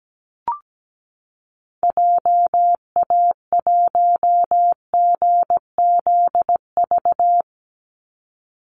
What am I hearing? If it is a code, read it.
Morse code, heard as a keyed tone, about 17 wpm, 700 Hz. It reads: JA1GZV
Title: Call Signs Repeated 2x - 17wpm